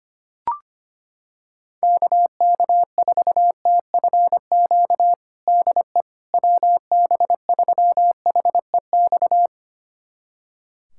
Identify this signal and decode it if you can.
Morse code, heard as a keyed tone, about 25 wpm, 700 Hz. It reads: KK4TFQ DE WB3HEX
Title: USA Callsign DE Callsign - 6 Characters - 25wpm